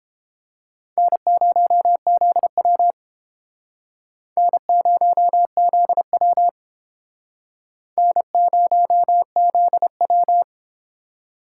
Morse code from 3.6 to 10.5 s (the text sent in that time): N0ZW N0ZW